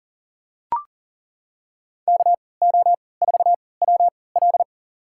Morse code, heard as a keyed tone, about 40 wpm, spaced as for 23 wpm, 700 Hz. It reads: KO4WL
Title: USA Call Signs - 5 Characters - 3x Character Spacing - 40wpm